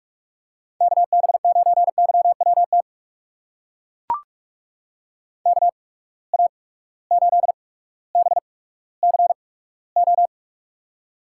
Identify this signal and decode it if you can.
Morse code, heard as a keyed tone, about 42 wpm, spaced as for 19 wpm, 700 Hz. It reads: KB9YWT KA8BCO
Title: USA Call Signs - 6 Characters - 8x Character Spacing - 45wpm